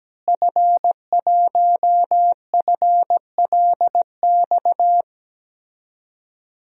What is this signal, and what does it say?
Morse code, heard as a keyed tone, about 17 wpm, 700 Hz. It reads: F1FLX
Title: DXCC-100 Call Signs - 5 Characters - Repeated 3x - 17wpm